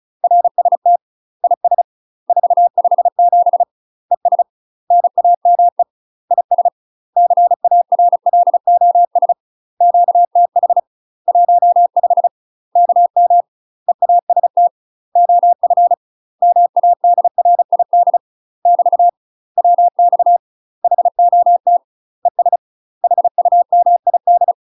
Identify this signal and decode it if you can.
Morse code, heard as a keyed tone, about 36 wpm, 700 Hz. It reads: RST IS 457 ES NAME IS CARLOS QTH 15 KM EAST OF MADRID <BT> WX HOT ES HUMID
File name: UK MASTER QSOs TRAINING / 35WPM / UK MASTER QSOs 0002